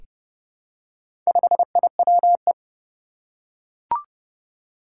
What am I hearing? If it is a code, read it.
Morse code, heard as a keyed tone, about 31 wpm, 700 Hz. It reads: 5IWE
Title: Four Challenging Letter-Number Combinations 30wpm